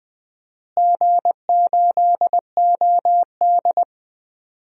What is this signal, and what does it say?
Morse code, heard as a keyed tone, about 20 wpm, 700 Hz. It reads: G8OD
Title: DXCC-100 Call Signs - 4 Characters - Repeated 3x - 20wpm